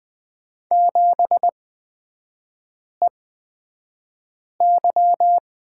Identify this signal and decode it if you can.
Morse code, heard as a keyed tone, about 20 wpm, spaced as for 5 wpm, 700 Hz. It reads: 7EY
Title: 250 Lesson 40e - BK - Three Characters 20wpm